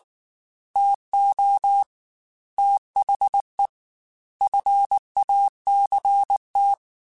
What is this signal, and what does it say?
Morse code, heard as a keyed tone, about 19 wpm, 800 Hz. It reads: TO THE FACT
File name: Technical Subjects / Technical Subjects-mp3s / AND Gate @18